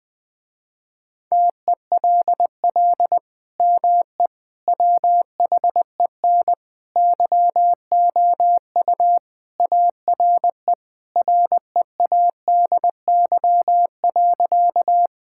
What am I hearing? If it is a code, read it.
Morse code, heard as a keyed tone, about 21 wpm, 700 Hz. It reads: TELL ME WHEN YOU ARE READY.
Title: Sentences with 2-Word Phrases - Spoken Beginning - 20wpm